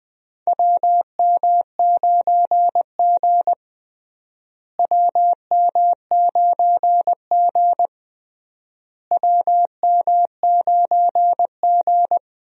Morse code, heard as a keyed tone, about 20 wpm, 700 Hz. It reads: WM9G WM9G WM9G
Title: USA Call Signs - 4 Characters - Repeated 3x - 20wpm